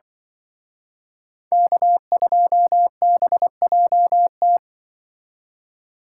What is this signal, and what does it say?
Morse code, heard as a keyed tone, about 24 wpm, 700 Hz. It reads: K2BJT
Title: Speed Racing - USA 1x3 Call Signs 20wpm